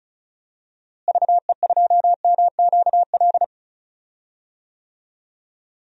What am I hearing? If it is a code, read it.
Morse code, heard as a keyed tone, about 36 wpm, 700 Hz. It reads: VE2MQL
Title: DXCC-100 Call Signs - 6 Characters - Repeated 3x - 35wpm